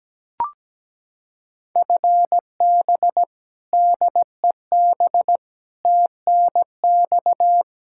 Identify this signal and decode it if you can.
Morse code, heard as a keyed tone, about 17 wpm, 700 Hz. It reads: FB DEB TNX
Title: Practice QSO - Exchange 3 - More Info - 17wpm